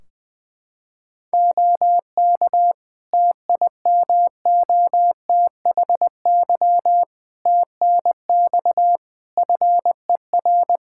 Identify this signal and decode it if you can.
Morse code, heard as a keyed tone, about 20 wpm, 700 Hz. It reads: OK TIMOTHY TNX FER
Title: K1USN SST QSO Practice 20wpm